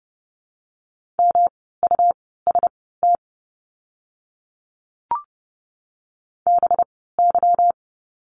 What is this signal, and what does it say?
Morse code, heard as a keyed tone, about 30 wpm, spaced as for 17 wpm, 700 Hz. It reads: MUST BY
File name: course / 30:10wpm / 237 Lesson 38f - Letter q - Words 30wpm